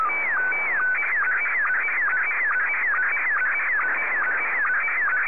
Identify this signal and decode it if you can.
SSTV (slow-scan television), Scottie 1.
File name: sstv / S1-VK7NMH